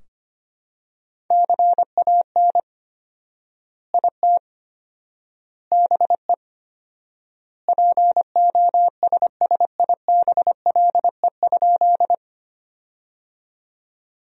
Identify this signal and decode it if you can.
Morse code, heard as a keyed tone, about 25 wpm, 700 Hz. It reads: CAN IT BE POSSIBLE?
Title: 4x Word-Spacing - Sentences from Top 500 Words - 25wpm